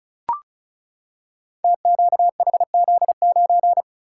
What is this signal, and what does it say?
Morse code, heard as a keyed tone, about 36 wpm, 700 Hz. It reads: TQHZ9
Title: Code Groups - 5 Letter:Numbers - 35wpm